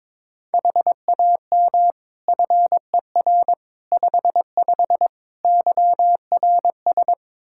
Morse code, heard as a keyed tone, about 22 wpm, 700 Hz. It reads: HAM FER 55 YRS
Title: Practice QSO - Exchange 3 - More Info - 22wpm